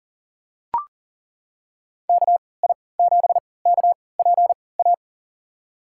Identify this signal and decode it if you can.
Morse code, heard as a keyed tone, about 40 wpm, spaced as for 23 wpm, 700 Hz. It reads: KI7KPA
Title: USA Call Signs - 6 Characters - 3x Character Spacing - 40wpm